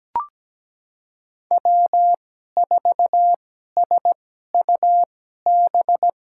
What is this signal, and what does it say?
Morse code, heard as a keyed tone, about 17 wpm, spaced as for 12 wpm, 700 Hz. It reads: W4SUB
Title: USA Call Signs - 5 Characters - 2x Character Spacing - 17wpm